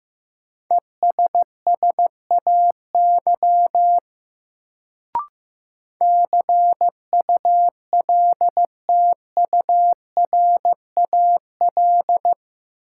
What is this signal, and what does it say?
Morse code, heard as a keyed tone, about 15 wpm, 700 Hz. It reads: ESSAY CULTURAL
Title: Sentences from Top 1500 Words - New Word Drill Alt - 15wpm